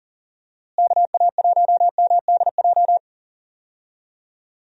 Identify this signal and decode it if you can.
Morse code, heard as a keyed tone, about 40 wpm, 700 Hz. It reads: KA1MDJ
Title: USA Call Signs - 6 Characters - Repeated 3x - 40wpm